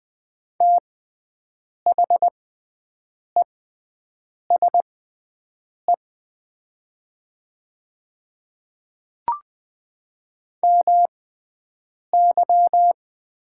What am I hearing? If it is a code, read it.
Morse code, heard as a keyed tone, about 20 wpm, spaced as for 7 wpm, 700 Hz. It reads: THESE MY
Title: 6x Character Spacing - Top 100 Words - 20wpm